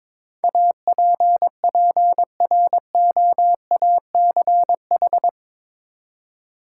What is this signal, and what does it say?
Morse code, heard as a keyed tone, about 22 wpm, 700 Hz. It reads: APPROACH
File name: practice / 22wpm / Eight Letter Words - 22wpm